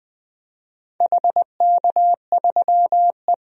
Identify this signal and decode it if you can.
Morse code, heard as a keyed tone, about 20 wpm, 700 Hz. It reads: HK3E
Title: DXCC-100 Call Signs - 4 Characters - 20wpm